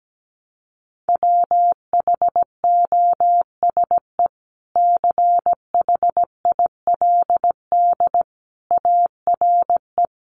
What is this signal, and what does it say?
Morse code, heard as a keyed tone, about 17 wpm, 700 Hz. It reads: WHOSE CHILD ARE
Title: Sentences from Top 1200 Words 17wpm